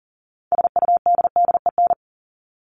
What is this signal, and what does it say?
Morse code, heard as a keyed tone, about 40 wpm, 700 Hz. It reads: SUDDEN
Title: LY Suffix - Top 100 Words with Base Words 40wpm